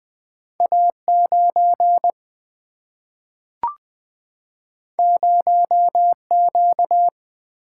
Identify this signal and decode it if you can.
Morse code, heard as a keyed tone, about 20 wpm, 700 Hz. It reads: A9 0Q
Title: Single-Letter Single-Number 20wpm